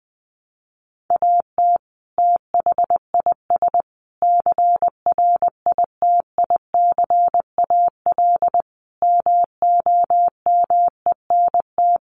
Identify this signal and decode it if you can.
Morse code, heard as a keyed tone, about 20 wpm, 700 Hz. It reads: AT THIS CRITICAL MOMENT
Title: Sentences from Top 1100 Words 20wpm